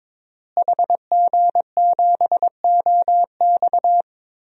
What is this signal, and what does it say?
Morse code, heard as a keyed tone, about 22 wpm, 700 Hz. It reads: HG7OX
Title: DXCC-100 Call Signs - 5 Characters - Repeated 2x - 22wpm